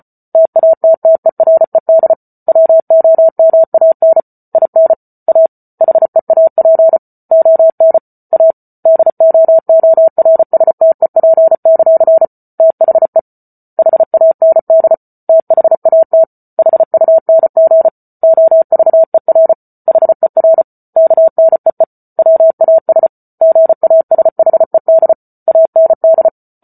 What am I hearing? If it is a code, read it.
Morse code, heard as a keyed tone, about 34 wpm, 650 Hz. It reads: TATTERED WOMAN IN A HEAP ON A DOORSTEP; THE HAND THAT HUNG OVER HER KNEE WAS GASHED AND